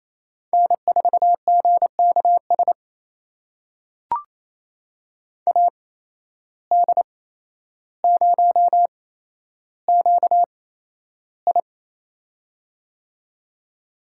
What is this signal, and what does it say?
Morse code, heard as a keyed tone, about 27 wpm, spaced as for 12 wpm, 700 Hz. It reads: N4GKS AD0QI
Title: USA Call Signs - 5 Characters - 8x Character Spacing - 28wpm